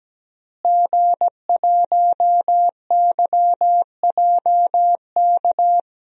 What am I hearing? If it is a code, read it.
Morse code, heard as a keyed tone, about 17 wpm, 700 Hz. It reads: G1YJK
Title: DXCC-100 Call Signs - 5 Characters - Repeated 2x - 17wpm